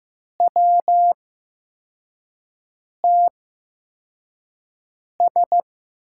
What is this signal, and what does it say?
Morse code, heard as a keyed tone, about 15 wpm, spaced as for 4 wpm, 700 Hz. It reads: WTS
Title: USA Call Signs - 8x Character Spacing - 15wpm